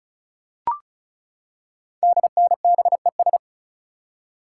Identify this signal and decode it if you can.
Morse code, heard as a keyed tone, about 36 wpm, 690 Hz. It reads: DNBES
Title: Code Groups - 5 Letters - 35wpm